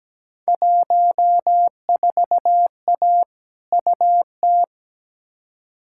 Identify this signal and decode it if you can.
Morse code, heard as a keyed tone, about 17 wpm, 700 Hz. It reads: 14A UT
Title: Field Day Exchanges Encore - 17wpm